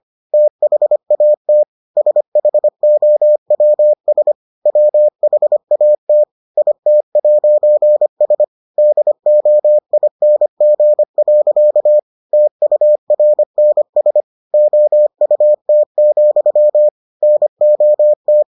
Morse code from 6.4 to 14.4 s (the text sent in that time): IT'S DOING. TURNS